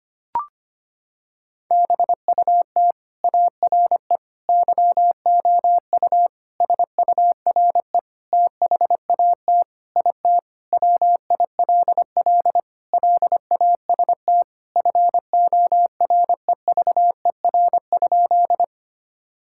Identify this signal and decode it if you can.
Morse code, heard as a keyed tone, about 25 wpm, 700 Hz. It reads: BUT ARE YOU SURE THAT IT WILL LAST FOREVER?
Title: Sentences from Top 1600 Words 25wpm